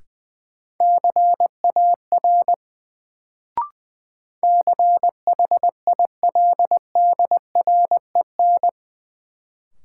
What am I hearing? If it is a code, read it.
Morse code, heard as a keyed tone, about 21 wpm, 700 Hz. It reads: CAR CHILDREN
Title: Sentences from Top 300 Words - New Word Drill - 20wpm